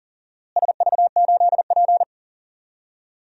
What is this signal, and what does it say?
Morse code, heard as a keyed tone, about 38 wpm, 700 Hz. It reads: SV8P
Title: DXCC-100 Call Signs - 4 Characters - Repeated 2x - 40wpm